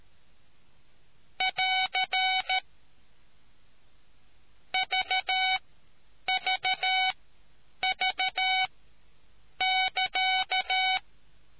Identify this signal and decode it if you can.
Morse code, heard as a keyed tone, about 13 wpm, spaced as for 8 wpm, 750 Hz. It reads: <AR> VVV<KA>